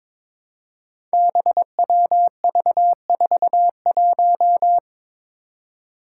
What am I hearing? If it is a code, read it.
Morse code, heard as a keyed tone, about 22 wpm, 700 Hz. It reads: BWV41
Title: Five Challenging Letter-Number Combinations 22wpm